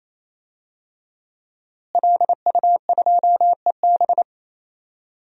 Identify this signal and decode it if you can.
Morse code, heard as a keyed tone, about 28 wpm, 700 Hz. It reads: LU2EB
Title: DXCC-100 Call Signs - 5 Characters - Repeated 3x - 28wpm